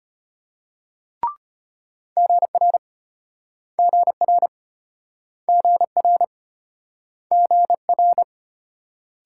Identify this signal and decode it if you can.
Morse code, heard as a keyed tone, about 34 wpm, 700 Hz. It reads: GR GR GR GR